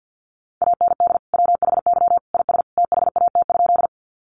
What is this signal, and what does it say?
Morse code, heard as a keyed tone, about 50 wpm, 700 Hz. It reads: AND WHY IS THAT?
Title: Sentences from Top 200 Words 50wpm